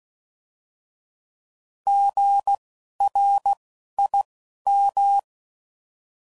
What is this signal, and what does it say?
Morse code, heard as a keyed tone, about 16 wpm, spaced as for 11 wpm, 800 Hz. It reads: GRIM
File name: Prefixes / By First Letter